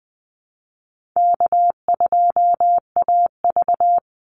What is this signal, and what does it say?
Morse code, heard as a keyed tone, about 20 wpm, 700 Hz. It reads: K2AV
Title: Easy Call Signs Repeated 3x - 20wpm